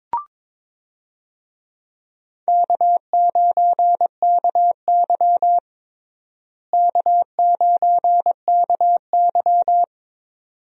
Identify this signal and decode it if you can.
Morse code, heard as a keyed tone, about 22 wpm, 700 Hz. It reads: K9KY K9KY
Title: USA Call Signs - 4 Characters - Repeated 2x - 22wpm